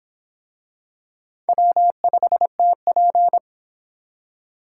Morse code, heard as a keyed tone, about 23 wpm, 700 Hz. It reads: W5TP